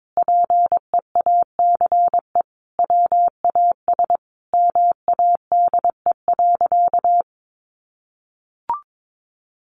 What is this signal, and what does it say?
Morse code, heard as a keyed tone, about 22 wpm, 700 Hz. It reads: PEACE WAS MADE.
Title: Sentences from Top 1500 Words 22wpm